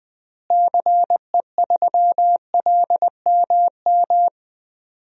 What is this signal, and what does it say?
Morse code, heard as a keyed tone, about 21 wpm, 700 Hz. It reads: CE3LMM
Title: DXCC-100 Call Signs - 6 Characters - Repeated 3x - 20wpm